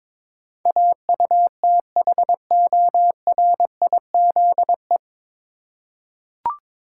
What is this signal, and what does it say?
Morse code, heard as a keyed tone, about 22 wpm, 700 Hz. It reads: AUTHORIZE
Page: Morse Code Ninja practice 22wpm